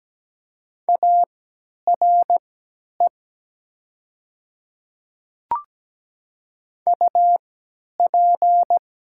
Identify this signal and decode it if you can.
Morse code, heard as a keyed tone, about 17 wpm, spaced as for 10 wpm, 700 Hz. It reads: ARE UP
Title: 3x Character Spacing - Top 100 Words - 17wpm